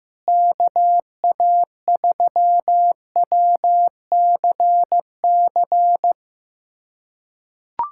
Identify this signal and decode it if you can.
Morse code, heard as a keyed tone, about 15 wpm, 700 Hz. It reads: KA3WCC
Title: USA Callsign DE Callsign - 6 Characters - 15wpm